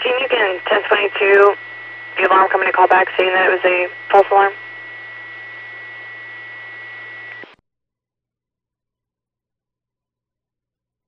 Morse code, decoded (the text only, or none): none